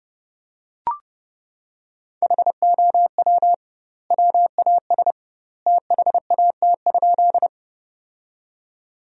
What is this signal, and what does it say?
Morse code, heard as a keyed tone, about 30 wpm, 700 Hz. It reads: HOW WAS THAT?
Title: 2x Word-Spacing - Sentences from Top 100 Words - 30wpm